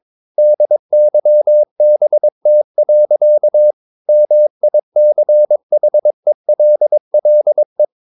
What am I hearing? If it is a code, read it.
Morse code, heard as a keyed tone, about 22 wpm, 600 Hz. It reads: DYBT. MICHELLE
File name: Ferie QRS 22wpm